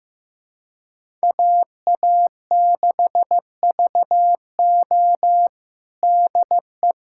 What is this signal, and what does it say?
Morse code, heard as a keyed tone, about 15 wpm, 700 Hz. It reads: AA6VO DE
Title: USA Callsign DE Callsign - 5 Characters - 15wpm